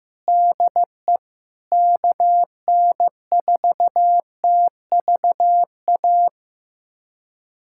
Morse code, heard as a keyed tone, about 15 wpm, 700 Hz. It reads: DE KN4TVA
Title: USA Callsign DE Callsign 15wpm